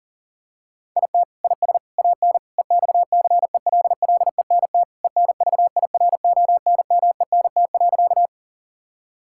Morse code, heard as a keyed tone, about 40 wpm, 700 Hz. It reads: IT IS AN EXCELLENT ENVIRONMENT.
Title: Sentences from Top 1000 Words - Review Part 5 - 40wpm